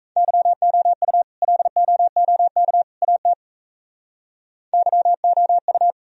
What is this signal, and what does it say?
Morse code, heard as a keyed tone, about 34 wpm, 700 Hz. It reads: YOU LOOK AT YOU
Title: Speed-Racing - Sets of 3 Words 28wpm